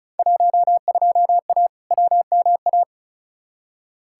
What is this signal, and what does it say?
Morse code, heard as a keyed tone, about 36 wpm, 700 Hz. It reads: 12A WMA
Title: Field Day Exchanges Encore - 35wpm